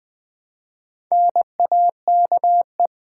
Morse code, heard as a keyed tone, about 20 wpm, 700 Hz. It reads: NAKE
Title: Build A Word - 1 Letter At A Time - 5-Letter Words - 20wpm